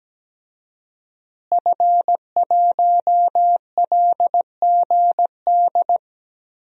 Morse code, heard as a keyed tone, about 17 wpm, 700 Hz. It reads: F1LGD
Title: DXCC-100 Call Signs - 5 Characters - Repeated 3x - 17wpm